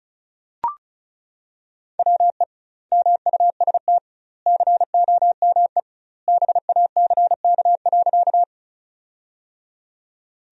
Morse code, heard as a keyed tone, about 34 wpm, 700 Hz. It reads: WE MUST COME BACK.
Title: 2x Word-Spacing - Sentences from Top 200 Words - 35wpm